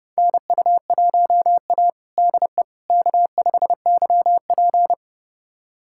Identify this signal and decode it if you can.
Morse code, heard as a keyed tone, about 30 wpm, 700 Hz. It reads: NU1A DE K5YP
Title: USA Callsign DE Callsign - 4 Characters - 30wpm